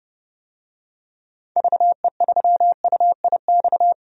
Morse code, heard as a keyed tone, about 31 wpm, 700 Hz. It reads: VE3UIX
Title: DXCC-100 Call Signs - 6 Characters - 30wpm